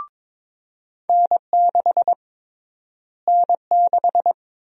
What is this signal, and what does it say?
Morse code, heard as a keyed tone, about 22 wpm, 700 Hz. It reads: N6 N6
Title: USA Call Sign Prefixes - Repeated 2x - 22wpm